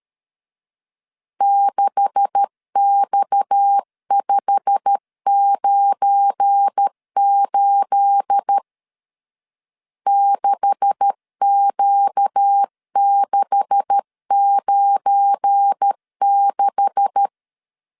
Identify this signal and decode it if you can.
Morse code, heard as a keyed tone, about 13 wpm, 800 Hz. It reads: 6X598 6Q696
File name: LEZIONI / 9 286 12wpm 2305